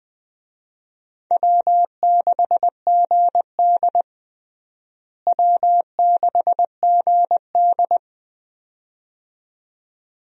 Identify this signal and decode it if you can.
Morse code, heard as a keyed tone, about 21 wpm, 700 Hz. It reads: W6GD W6GD
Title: USA Call Signs - 4 Characters - Repeated 2x - 20wpm